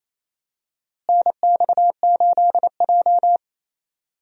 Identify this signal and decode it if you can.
Morse code, heard as a keyed tone, about 28 wpm, 700 Hz. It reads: NX8J
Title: USA Call Signs - 4 Characters - Repeated 3x - 28wpm